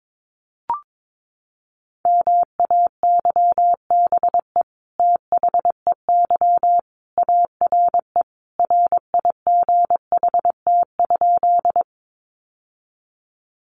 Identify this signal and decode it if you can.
Morse code, heard as a keyed tone, about 22 wpm, 700 Hz. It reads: MAYBE THEY ARE RIGHT?
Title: Sentences from Top 1000 Words 22wpm